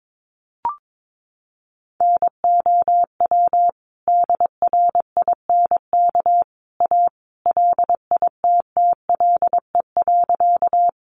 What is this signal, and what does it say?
Morse code, heard as a keyed tone, about 22 wpm, 700 Hz. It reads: NOW DRINK A LITTLE.
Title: Sentences from Top 1500 Words 22wpm